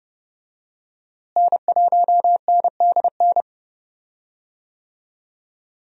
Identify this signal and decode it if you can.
Morse code, heard as a keyed tone, about 31 wpm, 700 Hz. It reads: N1NDN
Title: USA Call Signs - 5 Characters - Repeated 2x - 30wpm